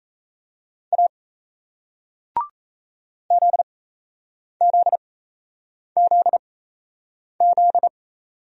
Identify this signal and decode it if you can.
Morse code, heard as a keyed tone, about 38 wpm, spaced as for 8 wpm, 700 Hz. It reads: A ZZZZ